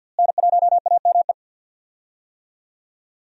Morse code, heard as a keyed tone, about 51 wpm, 700 Hz. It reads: N1AME
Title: USA Call Signs - 5 Characters - Repeated 2x - 50wpm